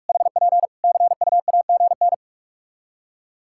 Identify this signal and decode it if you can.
Morse code, heard as a keyed tone, about 46 wpm, 700 Hz. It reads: HP CUAGN
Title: Practice QSO - Exchange 4 - Ending - 45wpm